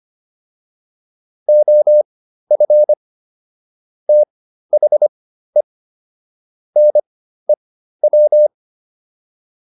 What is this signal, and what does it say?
Morse code, heard as a keyed tone, about 25 wpm, spaced as for 13 wpm, 600 Hz. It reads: OF THE NEW